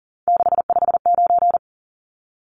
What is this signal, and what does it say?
Morse code, heard as a keyed tone, about 40 wpm, 700 Hz. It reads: 659
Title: Numbers - 3-Digit 40wpm